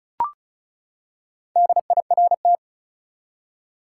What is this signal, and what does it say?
Morse code, heard as a keyed tone, about 36 wpm, 700 Hz. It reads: DIRT
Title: Four Letter Words - 35wpm